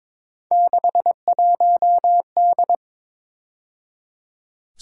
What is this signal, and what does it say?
Morse code, heard as a keyed tone, about 26 wpm, 700 Hz. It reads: 61D
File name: Speed-Racing - Three Letter-Number Combinations - 22wpm